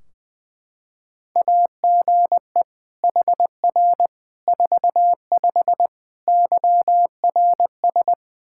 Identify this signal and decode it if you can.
Morse code, heard as a keyed tone, about 21 wpm, 700 Hz. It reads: AGE HR 45 YRS
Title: Practice QSO - Exchange 3 - More Info - 20wpm